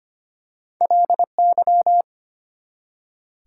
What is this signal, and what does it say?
Morse code, heard as a keyed tone, about 25 wpm, 700 Hz. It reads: LY
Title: Prefixes and Suffixes 25wpm